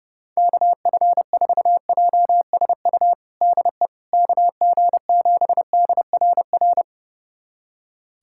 Morse code, heard as a keyed tone, about 30 wpm, 690 Hz. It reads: KF4JSU DE KG7DRR
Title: USA Callsign DE Callsign 30wpm